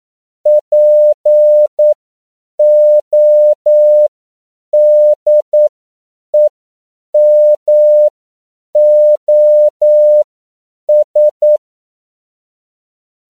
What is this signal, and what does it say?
Morse code, heard as a keyed tone, about 9 wpm, spaced as for 7 wpm, 600 Hz. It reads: PODEMOS
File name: CW / 10PPM